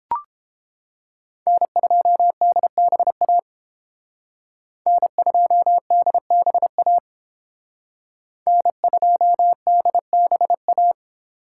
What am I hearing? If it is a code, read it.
Morse code, heard as a keyed tone, about 29 wpm, 700 Hz. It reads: N2DBA N2DBA N2DBA